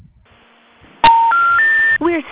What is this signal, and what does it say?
Special information tone, with speech.